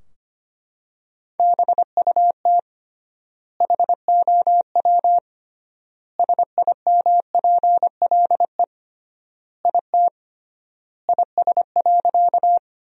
Morse code, heard as a keyed tone, about 25 wpm, 700 Hz. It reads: BUT HOW SIMPLE IT IS.
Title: 3x Word-Spacing - Sentences from Top 400 Words - 25wpm